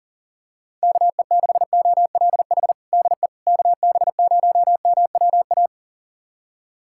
Morse code, heard as a keyed tone, about 40 wpm, 700 Hz. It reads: KE6OLH DE KB0MWA
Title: USA Callsign DE Callsign - 6 Characters - 40wpm